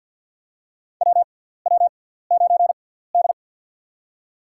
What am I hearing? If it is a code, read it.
Morse code, heard as a keyed tone, about 53 wpm, spaced as for 17 wpm, 690 Hz. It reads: WW9D
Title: USA Call Signs - 4 Characters - 6x Character Spacing - 50wpm